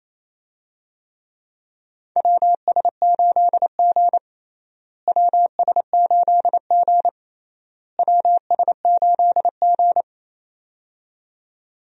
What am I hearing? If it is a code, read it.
Morse code, heard as a keyed tone, about 28 wpm, 700 Hz. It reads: WS8G WS8G WS8G
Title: USA Call Signs - 4 Characters - Repeated 3x - 28wpm